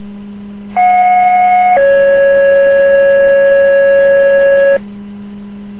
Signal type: two-tone page